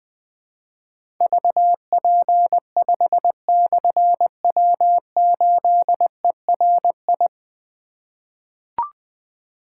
Morse code, heard as a keyed tone, about 21 wpm, 700 Hz. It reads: VP5/W8ERI
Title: Difficult Call Signs 20wpm